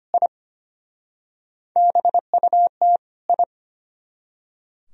Morse code, heard as a keyed tone, about 31 wpm, 700 Hz. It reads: I BUT I
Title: Speed-Racing - Sets of 2 Words 25wpm